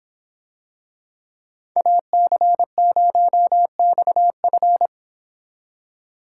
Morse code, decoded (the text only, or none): AC0XF